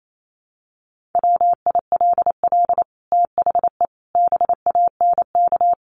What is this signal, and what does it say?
Morse code, heard as a keyed tone, about 28 wpm, 700 Hz. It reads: WILL THE BANK
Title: Sentences from Top 1500 Words 28wpm